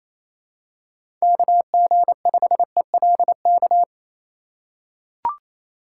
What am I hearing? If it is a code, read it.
Morse code, heard as a keyed tone, about 28 wpm, 700 Hz. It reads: KG5ELK
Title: USA Call Signs - 6 Characters - 2x Character Spacing - 28wpm